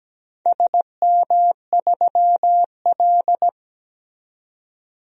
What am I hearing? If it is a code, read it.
Morse code, heard as a keyed tone, about 17 wpm, 700 Hz. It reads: SM3L
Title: DXCC-100 Call Signs - 4 Characters - 17wpm